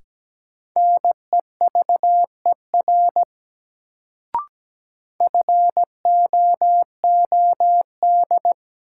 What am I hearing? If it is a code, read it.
Morse code, heard as a keyed tone, about 17 wpm, 700 Hz. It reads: NEVER FOOD
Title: Sentences from Top 300 Words - New Word Drill - 17wpm